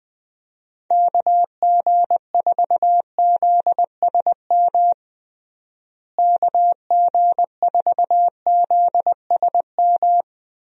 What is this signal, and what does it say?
Morse code, heard as a keyed tone, about 20 wpm, 700 Hz. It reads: KG4ZSM KG4ZSM
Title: USA Call Signs - 6 Characters - Repeated 2x - 20wpm